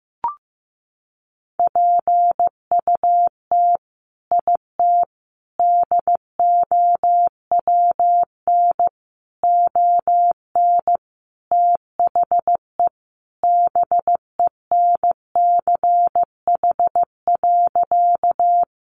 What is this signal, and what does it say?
Morse code, heard as a keyed tone, about 15 wpm, 700 Hz. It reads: PUT IT DOWN ON THE BENCH.
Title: Sentences from Top 1400 Words 15wpm